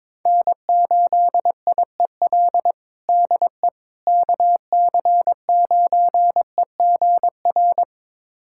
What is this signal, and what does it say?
Morse code, heard as a keyed tone, about 21 wpm, 700 Hz. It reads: N8IEL DE KC9EGR
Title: USA Callsign DE Callsign 22wpm